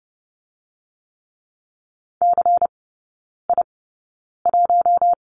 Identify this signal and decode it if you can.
Morse code, heard as a keyed tone, about 30 wpm, spaced as for 9 wpm, 700 Hz. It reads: CI1
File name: Lesson 19e - Letter w - Three Characters 30wpm